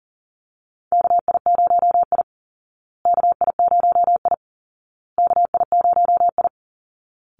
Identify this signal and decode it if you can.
Morse code, heard as a keyed tone, about 40 wpm, 700 Hz. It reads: KI0I KI0I KI0I
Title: Call Signs Repeated 2x - 40wpm